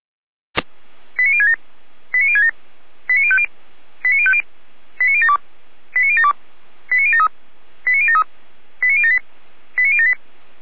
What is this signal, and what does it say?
Five-tone selective call (ZVEI), heard as quick runs of five tones, tones of about 80 ms. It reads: 89067 89067 89050 89050 89072 89072 89073 89073 89078 89078